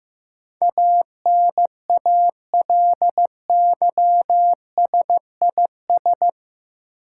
Morse code, heard as a keyed tone, about 15 wpm, 700 Hz. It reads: ANALYSIS
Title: Sentences from Top 1000 Words - Review Part 5 - New Word Drill - 15wpm